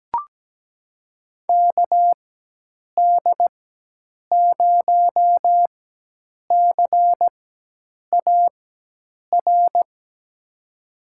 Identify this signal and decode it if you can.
Morse code, heard as a keyed tone, about 17 wpm, spaced as for 8 wpm, 700 Hz. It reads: KD0CAR
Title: USA Call Signs - 6 Characters - 4x Character Spacing - 17wpm